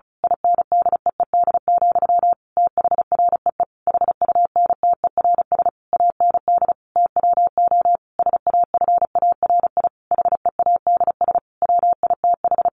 Morse code, heard as a keyed tone, about 34 wpm, 700 Hz. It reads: INDEED, THREE HUNTERS AND TWO SAFARI HEADS WITH
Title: A Sound Of Thunder 35wpm